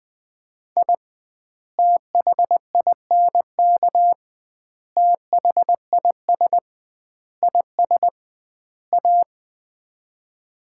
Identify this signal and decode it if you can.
Morse code, heard as a keyed tone, about 20 wpm, 700 Hz. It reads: I THINK THIS IS A
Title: 2x Word-Spacing - Sets of 5 Words - 20wpm